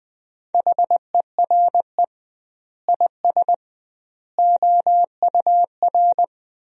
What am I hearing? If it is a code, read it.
Morse code, heard as a keyed tone, about 20 wpm, 700 Hz. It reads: HERE IS OUR
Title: 2x Word-Spacing - Sentences from Top 500 Words - 20wpm